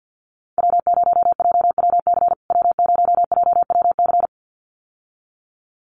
Morse code, heard as a keyed tone, about 50 wpm, 700 Hz. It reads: W0JWC W0JWC
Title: USA Call Signs Repeated 2x 50wpm